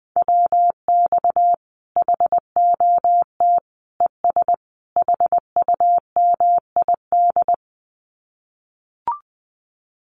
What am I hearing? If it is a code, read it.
Morse code, heard as a keyed tone, about 19 wpm, 700 Hz. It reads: WX HOT ES HUMID
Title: Common QSO Phrases 20wpm